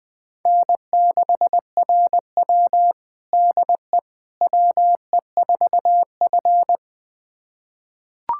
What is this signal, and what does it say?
Morse code, heard as a keyed tone, about 20 wpm, 700 Hz. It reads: N6RW DE WE4F
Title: USA Callsign DE Callsign - 4 Characters - 20wpm